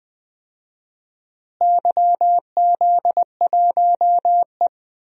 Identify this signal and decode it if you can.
Morse code, heard as a keyed tone, about 20 wpm, 700 Hz. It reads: YZ1E
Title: DXCC-100 Call Signs - 4 Characters - Repeated 2x - 20wpm